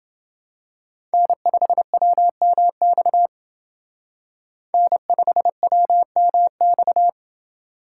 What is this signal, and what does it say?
Morse code, heard as a keyed tone, about 26 wpm, 700 Hz. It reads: N5WMX N5WMX